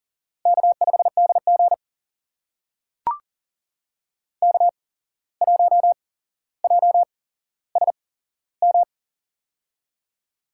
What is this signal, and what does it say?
Morse code, heard as a keyed tone, about 39 wpm, spaced as for 16 wpm, 700 Hz. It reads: K5DG K1JSM
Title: USA Call Signs - 8x Character Spacing - 40wpm